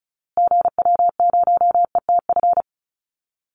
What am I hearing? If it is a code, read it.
Morse code, heard as a keyed tone, about 34 wpm, 700 Hz. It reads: GW0ETF
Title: Call Signs Repeated 2x - 35wpm